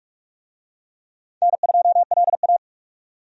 Morse code, heard as a keyed tone, about 44 wpm, 700 Hz. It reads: N2LA
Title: USA Call Signs - 4 Characters - 45wpm